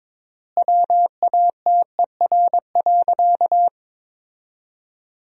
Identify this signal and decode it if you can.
Morse code, heard as a keyed tone, about 22 wpm, 700 Hz. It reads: WATER.
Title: 3x Word-Spacing - Sentences from Top 500 Words - 22wpm